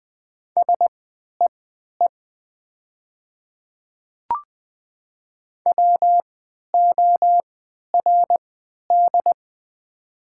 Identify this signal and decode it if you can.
Morse code, heard as a keyed tone, about 20 wpm, spaced as for 11 wpm, 700 Hz. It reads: SEE WORD
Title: 3x Character Spacing - Top 200 Words - 20wpm